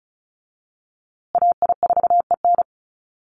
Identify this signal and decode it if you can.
Morse code, heard as a keyed tone, about 34 wpm, 700 Hz. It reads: AI4EN
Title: USA Call Signs Repeated 2x 35wpm